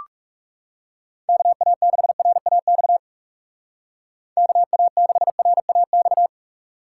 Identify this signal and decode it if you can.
Morse code, heard as a keyed tone, about 40 wpm, 700 Hz. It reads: KA6RAX KA6RAX